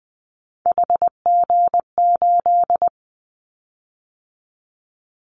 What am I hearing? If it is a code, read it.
Morse code, heard as a keyed tone, about 20 wpm, 700 Hz. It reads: HG8
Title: Callsign Prefixes for Top 25 DXCC - 20wpm